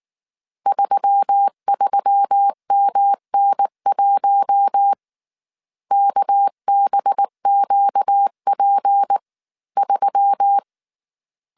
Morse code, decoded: 33MN1 KBQP3

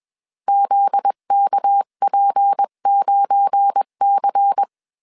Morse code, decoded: ZKP9C

